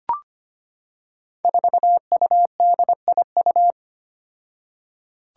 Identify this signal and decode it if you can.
Morse code, heard as a keyed tone, about 25 wpm, 700 Hz. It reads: 4UDIU
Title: Five Challenging Letter-Number Combinations 25wpm